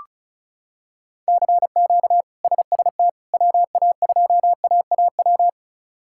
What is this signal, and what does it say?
Morse code, heard as a keyed tone, about 36 wpm, 700 Hz. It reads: CQ SST WA2AAW
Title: K1USN SST QSO Practice - 35wpm